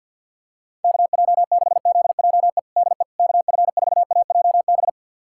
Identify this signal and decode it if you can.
Morse code, heard as a keyed tone, about 52 wpm, 700 Hz. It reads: KJ6ZJE DE KF4AJB
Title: USA Callsign DE Callsign 50wpm